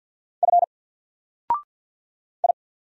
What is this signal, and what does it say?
Morse code, heard as a keyed tone, about 50 wpm, 700 Hz. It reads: F I
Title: Single Character (Letter:Number:Punctuation:Prosign) 50wpm